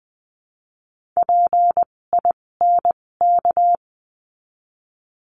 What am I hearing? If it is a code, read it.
Morse code, heard as a keyed tone, about 20 wpm, spaced as for 16 wpm, 700 Hz. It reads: PINK
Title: 256 Lesson 42a - Words 12-20wpm